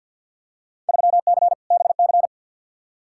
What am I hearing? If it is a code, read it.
Morse code, heard as a keyed tone, about 50 wpm, 700 Hz. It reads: 3C BC
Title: Field Day Exchanges Encore - 50wpm